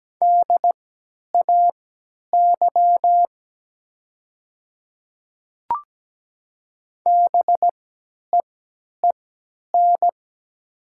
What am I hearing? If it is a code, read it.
Morse code, heard as a keyed tone, about 17 wpm, spaced as for 10 wpm, 700 Hz. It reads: DAY BEEN
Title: 3x Character Spacing - Top 200 Words - 17wpm